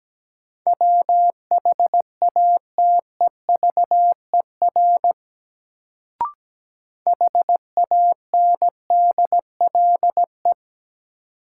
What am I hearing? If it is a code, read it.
Morse code, heard as a keyed tone, about 17 wpm, 700 Hz. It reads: WHATEVER HANDLE
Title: Sentences from Top 1100 Words - New Word Drill - 17wpm